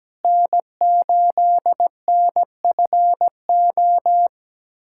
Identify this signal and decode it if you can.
Morse code, heard as a keyed tone, about 17 wpm, 700 Hz. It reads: N8NFO
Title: USA Callsign DE Callsign - 5 Characters - 17wpm